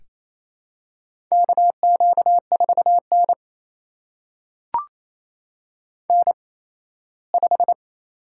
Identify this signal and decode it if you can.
Morse code, heard as a keyed tone, about 28 wpm, spaced as for 17 wpm, 700 Hz. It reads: KQ4N N5
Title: USA Call Signs - 4 Characters - 8x Character Spacing - 28wpm